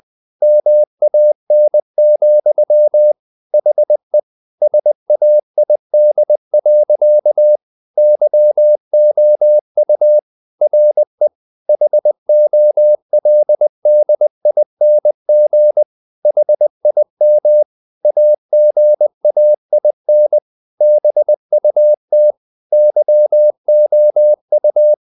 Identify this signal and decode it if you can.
Morse code, heard as a keyed tone, about 20 wpm, 600 Hz. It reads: MAN, HE SAID. YOU RE HOLDING HIM AGAIN BUT YOU
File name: the old man and the sea part 2 20wpm